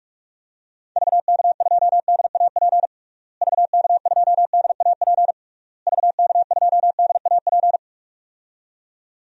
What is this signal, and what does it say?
Morse code, heard as a keyed tone, about 46 wpm, 700 Hz. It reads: VK2DAP VK2DAP VK2DAP
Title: DXCC-100 Call Signs - 6 Characters - Repeated 3x - 45wpm